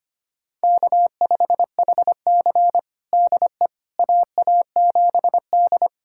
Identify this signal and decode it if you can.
Morse code, heard as a keyed tone, about 25 wpm, 700 Hz. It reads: K5HC DE AA7D
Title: USA Callsign DE Callsign - 4 Characters - 25wpm